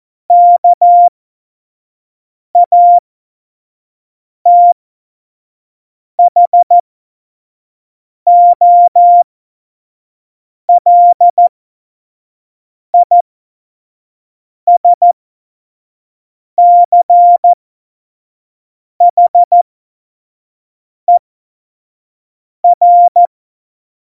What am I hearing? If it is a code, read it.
Morse code, heard as a keyed tone, about 14 wpm, spaced as for 5 wpm, 700 Hz. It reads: KATHOLISCHER
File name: Morsekurs / Witze / Witz 19. Januar 2021, 25 F70